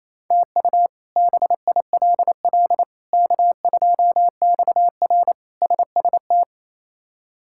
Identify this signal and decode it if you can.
Morse code, heard as a keyed tone, about 28 wpm, 700 Hz. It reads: TU BILL K2XR SST
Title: K1USN SST QSO Practice - 28wpm